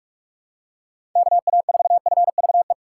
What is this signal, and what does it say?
Morse code, heard as a keyed tone, about 46 wpm, 700 Hz. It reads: KA4FVE
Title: USA Call Signs - 6 Characters - 4x Character Spacing - 45wpm